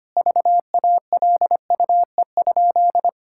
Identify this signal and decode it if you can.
Morse code, heard as a keyed tone, about 25 wpm, 700 Hz. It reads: VALUE?
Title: Sentences from Top 1000 Words - Review Part 6 - 25wpm